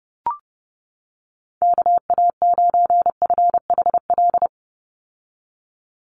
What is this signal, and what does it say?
Morse code, heard as a keyed tone, about 30 wpm, 700 Hz. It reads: KA9FHL
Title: USA Call Signs 30wpm